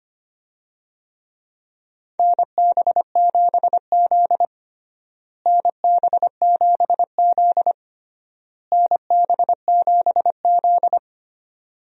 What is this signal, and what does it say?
Morse code, heard as a keyed tone, about 25 wpm, 700 Hz. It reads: NB7Z NB7Z NB7Z
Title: USA Call Signs - 4 Characters - Repeated 3x - 25wpm